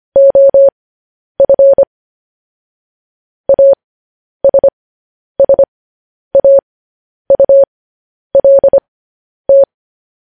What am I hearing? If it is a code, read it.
Morse code, heard as a keyed tone, about 25 wpm, spaced as for 10 wpm, 550 Hz. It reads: OF ASSAULT
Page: news-headlines-in-morse-code-at-10-wpm